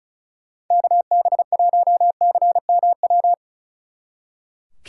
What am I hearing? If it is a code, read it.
Morse code, heard as a keyed tone, about 33 wpm, 700 Hz. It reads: KB1CMW
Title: Speed Racing - USA Call Signs 35wpm